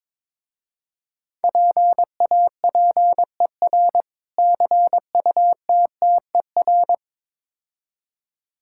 Morse code, heard as a keyed tone, about 22 wpm, 700 Hz. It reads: PAPER CUTTER
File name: Difficult Words with Hints - 22wpm